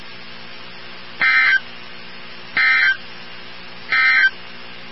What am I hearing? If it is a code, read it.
Emergency Alert System (SAME) tones, heard as harsh buzzing bursts. It is heard from 1.1 to 4.4 s.
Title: NOAA Weather Radio 162.550